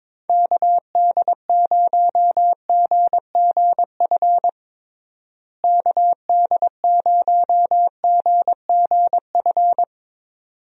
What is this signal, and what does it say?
Morse code, heard as a keyed tone, about 22 wpm, 700 Hz. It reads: KD0GGF KD0GGF
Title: USA Call Signs - 6 Characters - Repeated 2x - 22wpm